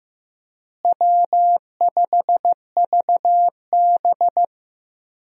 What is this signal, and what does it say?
Morse code, heard as a keyed tone, about 15 wpm, 700 Hz. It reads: W5VB
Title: USA Call Signs - 4 Characters - Repeated 2x - 15wpm